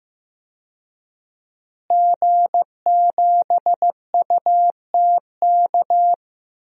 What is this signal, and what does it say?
Morse code, heard as a keyed tone, about 15 wpm, 700 Hz. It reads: G7UTK
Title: DXCC-100 Call Signs - 5 Characters - Repeated 2x - 15wpm